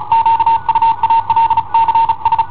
Morse code, heard as a keyed tone, about 36 wpm, 910 Hz. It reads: QUALCS
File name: i7ohp / cassetta i7ohp B